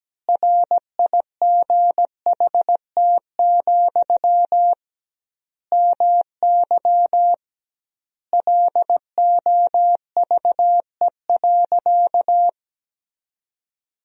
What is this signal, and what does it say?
Morse code, heard as a keyed tone, about 17 wpm, 700 Hz. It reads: RIGHT, MY LOVE.
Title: 2x Word-Spacing - Sentences from Top 400 Words - 17wpm